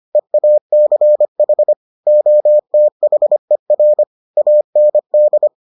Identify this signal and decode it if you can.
Morse code, heard as a keyed tone, about 25 wpm, 600 Hz. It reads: EACH OTHER AND